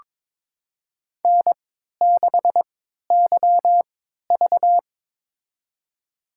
Morse code, heard as a keyed tone, about 23 wpm, spaced as for 13 wpm, 700 Hz. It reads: N6YV
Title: USA Call Signs - 4 Characters - 3x Character Spacing - 22wpm